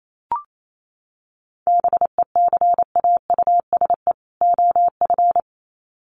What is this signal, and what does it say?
Morse code, heard as a keyed tone, about 28 wpm, 700 Hz. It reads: BECAUSE OF
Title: Sets of 2 Words 28wpm